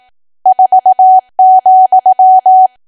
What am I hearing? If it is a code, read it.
Morse code, heard as a keyed tone, about 18 wpm, 750 Hz. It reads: 4,
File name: cwa / W1AW-Level2 / 18WPM